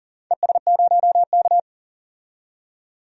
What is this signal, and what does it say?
Morse code, heard as a keyed tone, about 37 wpm, 700 Hz. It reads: ES0K
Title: DXCC-100 Call Signs - 4 Characters - 40wpm